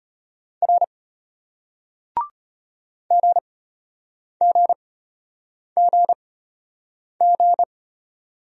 Morse code, heard as a keyed tone, about 34 wpm, spaced as for 8 wpm, 700 Hz. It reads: R GGGG